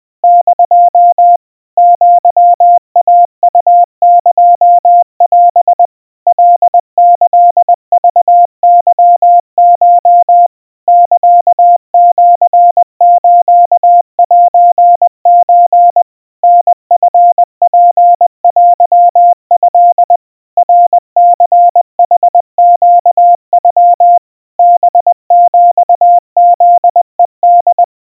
Morse code (Japanese Wabun code), heard as a keyed tone, about 21 wpm, 700 Hz. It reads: <DO> アイウエオ カキクケコ サシスセソ タチツテト ナニヌネノ ハヒフヘホ
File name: kawa / Morse sample / aiueo 70 cpm 700 Hz Crispy